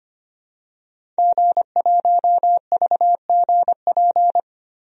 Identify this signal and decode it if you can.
Morse code, heard as a keyed tone, about 25 wpm, 700 Hz. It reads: G1VGP
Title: DXCC-100 Call Signs - 5 Characters - 25wpm